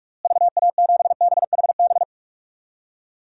Morse code, heard as a keyed tone, about 44 wpm, 700 Hz. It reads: VA7BHB
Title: DXCC-100 Call Signs - 6 Characters - Repeated 3x - 45wpm